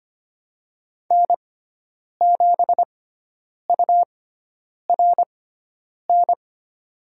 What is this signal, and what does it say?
Morse code, heard as a keyed tone, about 25 wpm, spaced as for 9 wpm, 700 Hz. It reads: N7URN
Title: USA Call Signs - 5 Characters - 6x Character Spacing - 25wpm